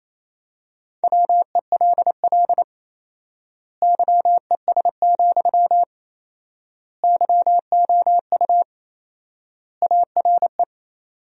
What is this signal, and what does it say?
Morse code, heard as a keyed tone, about 28 wpm, 700 Hz. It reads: WELL YES, YOU ARE
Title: 4x Word-Spacing - Sentences from Top 500 Words - 28wpm